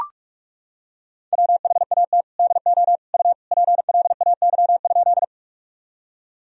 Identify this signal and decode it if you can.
Morse code, heard as a keyed tone, about 41 wpm, 700 Hz. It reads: WHAT DO U PLAY?
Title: Sentences from Top 200 Words and Abbreviations 45wpm